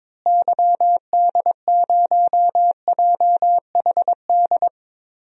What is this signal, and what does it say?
Morse code, heard as a keyed tone, about 22 wpm, 700 Hz. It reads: YD0JHD
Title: DXCC-100 Call Signs - 6 Characters - Repeated 2x - 22wpm